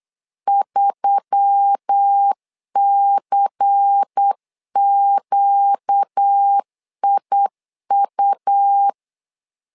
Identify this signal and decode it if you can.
Morse code, heard as a keyed tone, about 8 wpm, 800 Hz. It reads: 3CQIU
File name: LEZIONI / 5 LBVC / 5 LBVC 8wpm 2935